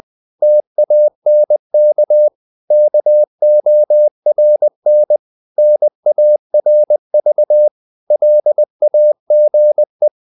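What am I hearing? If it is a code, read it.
Morse code, heard as a keyed tone, about 20 wpm, 600 Hz. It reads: TANK KORN NARV LAGE
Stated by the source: Morse code practice of short Danish words at 20 wpm, from CW med Gnister.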